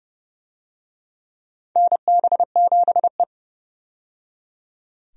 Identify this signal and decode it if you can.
Morse code, heard as a keyed tone, about 30 wpm, 700 Hz. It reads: NB7E
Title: USA Call Signs - 4 Characters - 30wpm